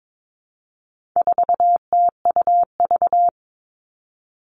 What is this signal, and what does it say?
Morse code, heard as a keyed tone, about 22 wpm, 700 Hz. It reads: 4TUV
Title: All Cut Numbers - 4-Digit 22wpm